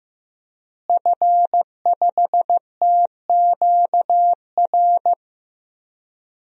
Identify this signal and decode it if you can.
Morse code, heard as a keyed tone, about 15 wpm, 700 Hz. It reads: F5TQR
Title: DXCC-100 Call Signs - 5 Characters - Repeated 3x - 15wpm